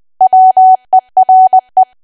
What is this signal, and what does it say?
Morse code, heard as a keyed tone, about 20 wpm, 750 Hz. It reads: WERE
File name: cwa / W1AW-Level3 / 20WPM